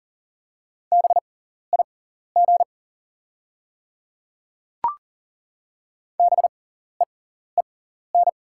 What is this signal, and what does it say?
Morse code, heard as a keyed tone, about 40 wpm, spaced as for 14 wpm, 700 Hz. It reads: BIG BEEN
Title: 6x Character Spacing - Top 500 Words - 40wpm